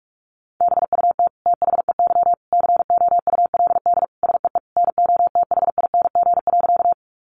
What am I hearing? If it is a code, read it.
Morse code, heard as a keyed tone, about 44 wpm, 700 Hz. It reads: BUT THEY COULD SEE NOTHING.
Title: Sentences from Top 500 Words 45wpm